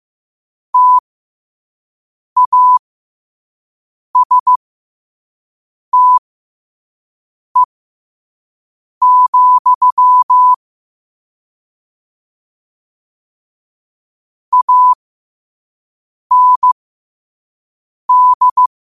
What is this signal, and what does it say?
Morse code, heard as a keyed tone, about 15 wpm, spaced as for 5 wpm, 1000 Hz. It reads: TASTE, AND
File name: qotdfiles / 5wpm